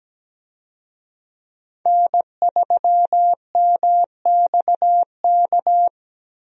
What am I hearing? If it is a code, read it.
Morse code, heard as a keyed tone, about 17 wpm, 700 Hz. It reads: N3MXK
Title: USA Call Signs - 5 Characters - 17wpm